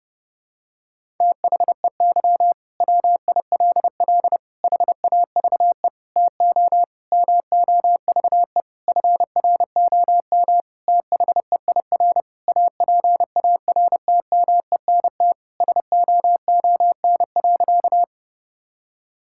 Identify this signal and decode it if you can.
Morse code, heard as a keyed tone, about 30 wpm, 700 Hz. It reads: THEY WILL HAVE TO MOVE FROM THEIR APARTMENT SOON.
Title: Sentences from Top 1000 Words - Review Part 6 - 30wpm